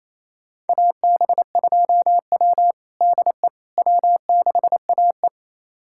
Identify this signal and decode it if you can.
Morse code, heard as a keyed tone, about 28 wpm, 700 Hz. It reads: AB2W DE W6AE
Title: USA Callsign DE Callsign - 4 Characters - 28wpm